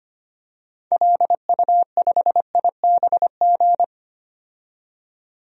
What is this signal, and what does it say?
Morse code, heard as a keyed tone, about 25 wpm, 700 Hz. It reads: LU5IBG
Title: DXCC-100 Call Signs - 6 Characters - 25wpm